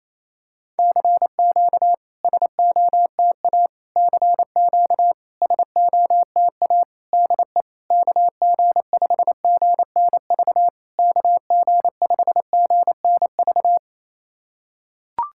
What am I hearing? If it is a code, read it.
Morse code, heard as a keyed tone, about 27 wpm, 660 Hz. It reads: CQ SOTA CQ SOTA DE KG5GNV KG5GNV
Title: SOTA QSO Practice 28wpm